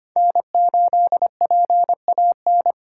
Morse code, heard as a keyed tone, about 25 wpm, 700 Hz. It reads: N8PAN
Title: USA Call Signs - 5 Characters - 4x Character Spacing - 25wpm